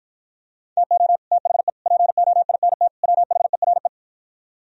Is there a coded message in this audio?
Yes: Morse code from 0.8 to 3.9 s, heard as a keyed tone, about 52 wpm, 700 Hz. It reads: TO THE POINT WHERE